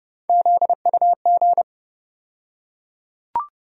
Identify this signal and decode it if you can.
Morse code, heard as a keyed tone, about 30 wpm, 700 Hz. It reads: ZUG
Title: USA Call Sign Postfixes - 3 Letters 30wpm